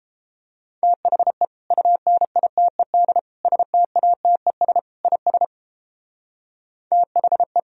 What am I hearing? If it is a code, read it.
Morse code, heard as a keyed tone, about 32 wpm, 700 Hz. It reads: THE UNITED STATES IS THE